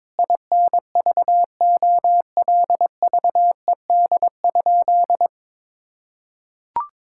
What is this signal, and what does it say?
Morse code, heard as a keyed tone, about 22 wpm, 700 Hz. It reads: INVOLVED?
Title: Sentences from Top 1000 Words - Review Part 5 - 22wpm